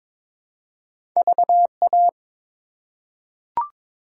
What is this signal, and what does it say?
Morse code, heard as a keyed tone, about 22 wpm, 700 Hz. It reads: VA
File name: practice / 22wpm / ARRL Sections v2 - 22wpm